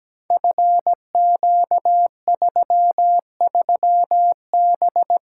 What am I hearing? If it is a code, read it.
Morse code, heard as a keyed tone, about 17 wpm, 700 Hz. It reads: FQ33B
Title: Code Groups - 5 Letter:Numbers - 17wpm